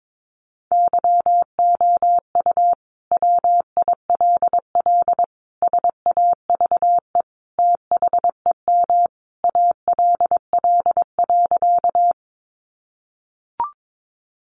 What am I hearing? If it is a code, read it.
Morse code, heard as a keyed tone, about 22 wpm, 700 Hz. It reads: YOU WILL SAVE THEM ALL.
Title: Sentences from Top 1100 Words 22wpm